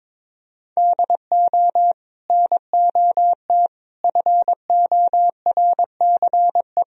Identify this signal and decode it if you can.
Morse code, heard as a keyed tone, about 22 wpm, 700 Hz. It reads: DO NOT FORCE
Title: Sentences from Top 1000 Words - Review Part 3 - 22wpm